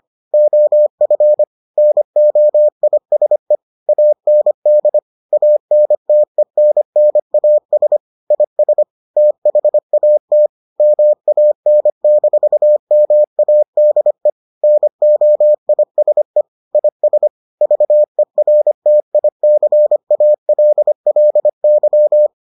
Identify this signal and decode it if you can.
Morse code, heard as a keyed tone, about 25 wpm, 600 Hz. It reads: OF NOISE AND ANTENNAS IS THAT MAN-MADE NOISE IS VERTICALLY